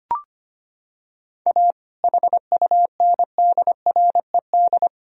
Morse code, heard as a keyed tone, about 25 wpm, 700 Hz. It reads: A HUNDRED
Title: Sentences from Top 1000 Words - Review Part 2 - 25wpm